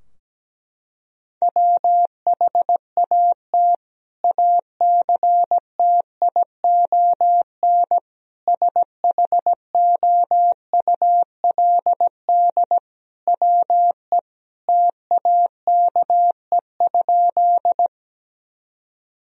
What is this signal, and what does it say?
Morse code, heard as a keyed tone, about 17 wpm, 700 Hz. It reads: WHAT ACTION SHOULD WE TAKE?
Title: Sentences from Top 1000 Words - Review Part 4 - 17wpm